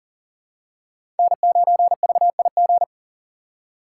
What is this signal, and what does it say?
Morse code, heard as a keyed tone, about 40 wpm, 700 Hz. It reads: N9VIG
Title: USA Call Signs - 3x Character Spacing - 40wpm